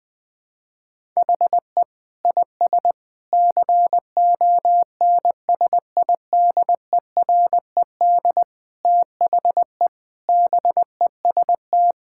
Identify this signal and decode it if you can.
Morse code, heard as a keyed tone, about 20 wpm, 700 Hz. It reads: HE IS CONSIDERED THE BEST
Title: Sentences with 6-Word Phrases - 20wpm